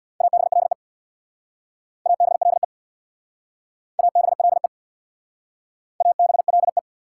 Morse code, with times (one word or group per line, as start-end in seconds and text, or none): 0.2-0.7 ABLE
2.1-2.6 ABLE
4.0-4.7 ABLE
6.0-6.8 ABLE